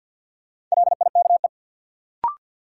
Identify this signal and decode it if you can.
Morse code, heard as a keyed tone, about 53 wpm, 700 Hz. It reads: LIKE
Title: Sentences from Top 100 Words - New Word Drill - 50wpm